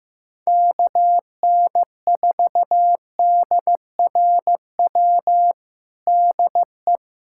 Morse code, heard as a keyed tone, about 15 wpm, 700 Hz. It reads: KN4DRW DE
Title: USA Callsign DE Callsign 15wpm